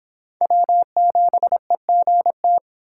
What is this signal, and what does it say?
Morse code, heard as a keyed tone, about 25 wpm, 700 Hz. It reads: W7EGT